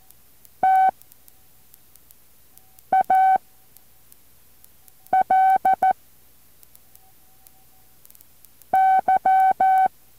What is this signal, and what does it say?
Morse code, heard as a keyed tone, about 14 wpm, 760 Hz. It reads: T A L Y
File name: intl morse code2